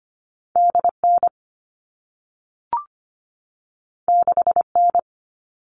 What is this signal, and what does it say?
Morse code, heard as a keyed tone, about 25 wpm, 700 Hz. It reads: DN 6N
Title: All Cut Numbers - 2-Digit 25wpm